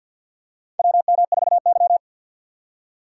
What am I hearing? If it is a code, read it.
Morse code, heard as a keyed tone, about 47 wpm, 700 Hz. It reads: WM4Y
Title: USA Call Signs - 3x Character Spacing - 50wpm